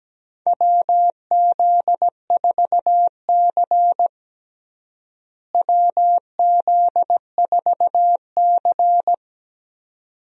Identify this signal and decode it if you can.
Morse code, heard as a keyed tone, about 17 wpm, 700 Hz. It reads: WZ4C WZ4C
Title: USA Call Signs - 4 Characters - Repeated 2x - 17wpm